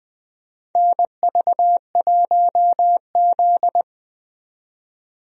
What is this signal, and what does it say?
Morse code, heard as a keyed tone, about 20 wpm, 700 Hz. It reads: NV1Z
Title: USA Call Signs - 4 Characters - 20wpm